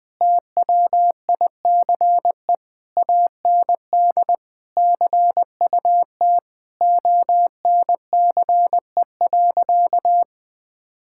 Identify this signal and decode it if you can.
Morse code, heard as a keyed tone, about 20 wpm, 700 Hz. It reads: TWICE AND CUT ONCE.
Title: Difficult Sentences with Hints - 20wpm